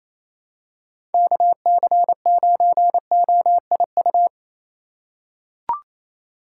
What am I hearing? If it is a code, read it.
Morse code, heard as a keyed tone, about 28 wpm, 700 Hz. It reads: KC9OIU
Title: USA Call Signs - 6 Characters - 3x Character Spacing - 28wpm